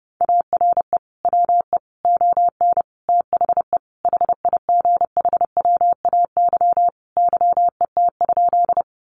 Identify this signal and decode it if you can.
Morse code, heard as a keyed tone, about 30 wpm, 700 Hz. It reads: ARE WE ON THE HIGHWAY YET?
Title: Sentences from Top 1400 Words 30wpm